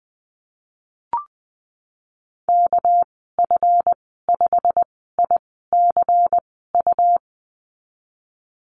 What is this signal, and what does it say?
Morse code, heard as a keyed tone, about 20 wpm, spaced as for 15 wpm, 700 Hz. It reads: KF5ICU
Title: 238 Lesson 38g - Letter q - Call Signs 20wpm